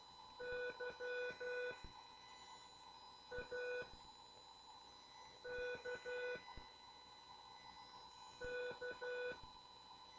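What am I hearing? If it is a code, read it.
Morse code, heard as a keyed tone, about 12 wpm, 490 Hz. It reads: Y A K K
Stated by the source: CW class lesson from The Martian Net with instructor N2KPS.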